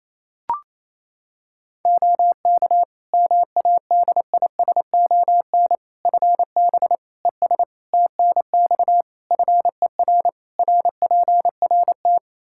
Morse code, heard as a keyed tone, about 28 wpm, 700 Hz. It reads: OK MADISON FB ES TNX FER RPRT
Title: Practice QSO - Exchange 2 - Info - 28wpm